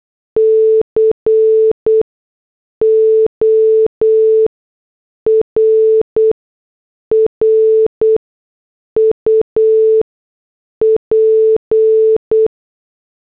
Morse code, heard as a keyed tone, about 8 wpm, spaced as for 6 wpm, 440 Hz. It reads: CORRUP